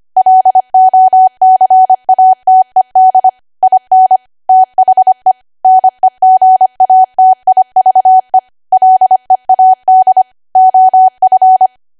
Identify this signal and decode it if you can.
Morse code, heard as a keyed tone, about 25 wpm, 750 Hz. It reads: LOCATED IN THE NEGATIVE LEAD OF
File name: Archive / 25 WPM / 25WPM